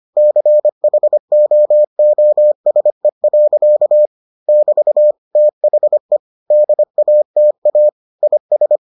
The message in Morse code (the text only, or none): CHOOSE. <BT> THE DATA IS